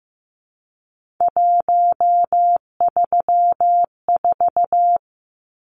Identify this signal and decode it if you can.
Morse code, heard as a keyed tone, about 15 wpm, 700 Hz. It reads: 134
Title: Numbers - 3-Digit 15wpm